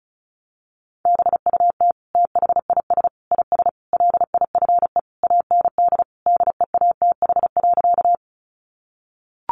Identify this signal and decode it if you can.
Morse code, heard as a keyed tone, about 34 wpm, 700 Hz. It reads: BUT THIS IS LIFE AND DEATH.
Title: Sentences from Top 1100 Words 35wpm